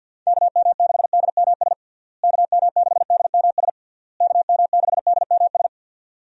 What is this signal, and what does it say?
Morse code, heard as a keyed tone, about 49 wpm, 700 Hz. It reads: KM6DMS KM6DMS KM6DMS
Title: USA Call Signs - 6 Characters - Repeated 3x - 50wpm